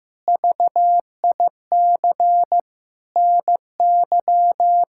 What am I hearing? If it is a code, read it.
Morse code, heard as a keyed tone, about 15 wpm, 700 Hz. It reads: VIC NY
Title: K1USN SST QSO Practice - 15wpm